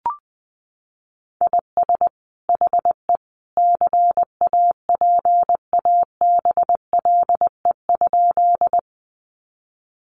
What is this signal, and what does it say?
Morse code, heard as a keyed tone, about 20 wpm, 700 Hz. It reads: IS HE CAPABLE?
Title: Sentences from Top 1200 Words 20wpm